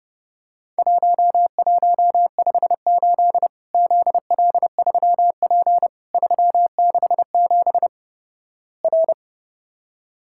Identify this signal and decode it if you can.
Morse code, heard as a keyed tone, about 32 wpm, 700 Hz. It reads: 1158 ZL3P 367 R
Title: Worked All Europe DX Contest - QTC Exchange - 30wpm